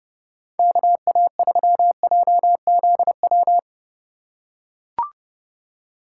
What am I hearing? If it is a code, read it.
Morse code, heard as a keyed tone, about 30 wpm, 700 Hz. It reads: KA3JZW
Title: USA Call Signs - 6 Characters - 3x Character Spacing - 30wpm